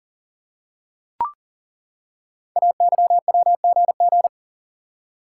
Morse code, heard as a keyed tone, about 39 wpm, 690 Hz. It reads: AYWGG
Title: Code Groups - 5 Letters - 40wpm